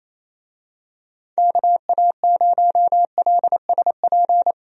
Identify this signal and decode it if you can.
Morse code, heard as a keyed tone, about 28 wpm, 700 Hz. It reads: KA0LSP
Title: USA Call Signs - 6 Characters - 28wpm